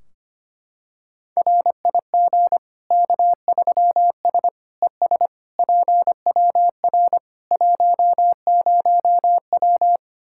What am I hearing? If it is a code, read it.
Morse code, heard as a keyed tone, about 25 wpm, 700 Hz. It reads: RIG K3S ES PWR 10W
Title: Practice QSO - Exchange 2 - Info - 25wpm